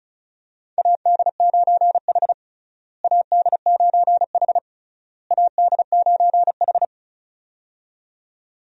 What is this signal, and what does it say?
Morse code, heard as a keyed tone, about 34 wpm, 700 Hz. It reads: AD9H AD9H AD9H
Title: USA Call Signs - 4 Characters - Repeated 3x - 35wpm